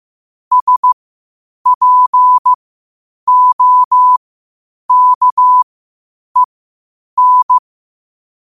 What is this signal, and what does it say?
Morse code, heard as a keyed tone, about 15 wpm, spaced as for 8 wpm, 1000 Hz. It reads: SPOKEN
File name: qotdfiles / 8wpm